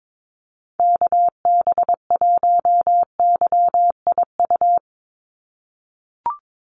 Morse code, heard as a keyed tone, about 22 wpm, 700 Hz. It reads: KB1YIU
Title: USA Call Signs 22wpm